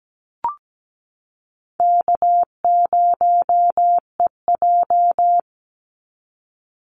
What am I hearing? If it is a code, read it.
Morse code, heard as a keyed tone, about 17 wpm, 700 Hz. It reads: K0EJ
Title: Easy Call Signs Repeated 3x - 17wpm